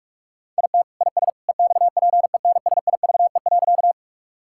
Morse code, heard as a keyed tone, about 42 wpm, 700 Hz. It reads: IT IS EXPENSIVE.
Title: Sentences from Top 1000 Words - Review Part 6 - 45wpm